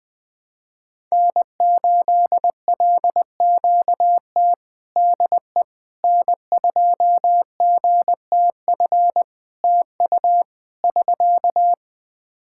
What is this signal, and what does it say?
Morse code, heard as a keyed tone, about 20 wpm, 700 Hz. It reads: N8LQT DE N2GTF TU <SK>
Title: K1USN SST QSO Practice 20wpm